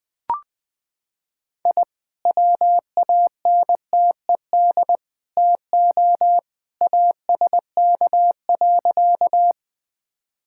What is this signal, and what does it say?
Morse code, heard as a keyed tone, about 21 wpm, 700 Hz. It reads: I WANTED TO ASK.
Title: Sentences with 3-Word Phrases - 20wpm